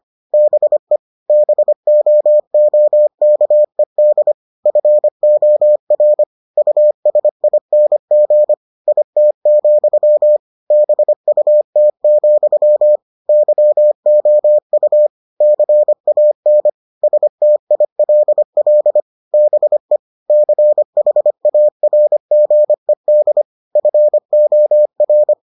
Morse code, heard as a keyed tone, about 25 wpm, 600 Hz. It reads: BE BOOKED FOR USING IT, BUT, YOU CAN STILL BE CHARGED FOR